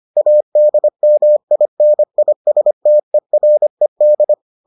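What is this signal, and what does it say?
Morse code, heard as a keyed tone, about 25 wpm, 600 Hz. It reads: ADMINISTERED
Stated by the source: Morse edition of the Foundations of Amateur Radio episodes.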